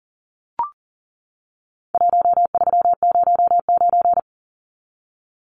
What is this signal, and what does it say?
Morse code, heard as a keyed tone, about 40 wpm, 700 Hz. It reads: 1309
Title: Numbers - 4-Digit 40wpm